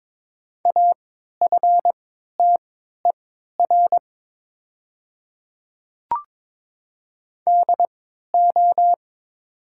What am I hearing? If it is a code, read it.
Morse code, heard as a keyed tone, about 22 wpm, spaced as for 12 wpm, 700 Hz. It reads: AFTER DO
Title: 3x Character Spacing - Top 500 Words - 22wpm